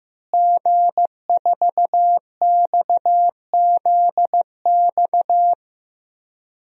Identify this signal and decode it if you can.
Morse code, heard as a keyed tone, about 16 wpm, 700 Hz. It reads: G4XZX
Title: DXCC-100 Call Signs - 5 Characters - Repeated 3x - 15wpm